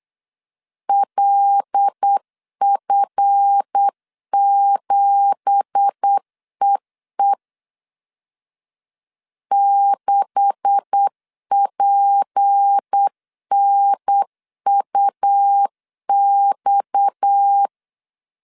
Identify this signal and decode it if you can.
Morse code, heard as a keyed tone, about 8 wpm, 800 Hz. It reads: LF7EE 6PNUX
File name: CW / DET / 1 9 1500C S A 8wpm 5637